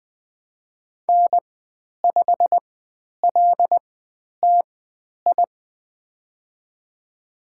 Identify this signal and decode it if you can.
Morse code, heard as a keyed tone, about 20 wpm, spaced as for 10 wpm, 700 Hz. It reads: N5LTI
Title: 095 Lesson 16g - Letter c - Call Signs 20wpm